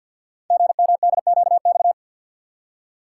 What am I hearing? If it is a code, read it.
Morse code, heard as a keyed tone, about 48 wpm, 700 Hz. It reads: GMDQX
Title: Code Groups - 5 Letters - 50wpm